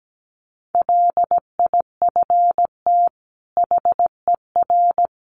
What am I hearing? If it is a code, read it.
Morse code, heard as a keyed tone, about 17 wpm, 700 Hz. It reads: LIFT HER
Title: Sentences from Top 1300 Words 17wpm